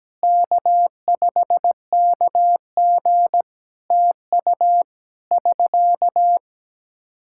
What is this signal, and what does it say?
Morse code, heard as a keyed tone, about 17 wpm, 700 Hz. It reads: K5KG TU <SK>
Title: K1USN SST QSO Practice 17wpm